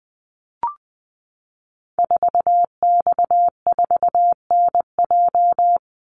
Morse code, heard as a keyed tone, about 20 wpm, 700 Hz. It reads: 4X4NJ
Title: Call Signs - 20wpm